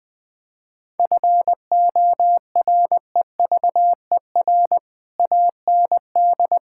Morse code, heard as a keyed tone, about 20 wpm, 700 Hz. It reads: FOREVER AND
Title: Sentences from Top 1600 Words 20wpm